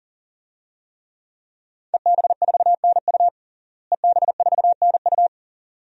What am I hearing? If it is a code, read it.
Morse code, heard as a keyed tone, about 37 wpm, 700 Hz. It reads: EB4NU EB4NU
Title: DXCC-100 Call Signs - 5 Characters - Repeated 2x - 40wpm